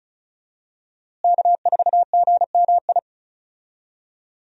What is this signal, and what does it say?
Morse code, heard as a keyed tone, about 36 wpm, 700 Hz. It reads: K4GMI
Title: USA Call Signs - 5 Characters - Repeated 2x - 35wpm